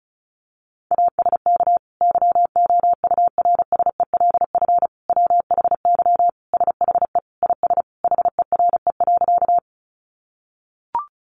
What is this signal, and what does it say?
Morse code, heard as a keyed tone, about 34 wpm, 700 Hz. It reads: ASK YOURSELF WHY SHE IS HERE.
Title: Sentences from Top 600 Words 35wpm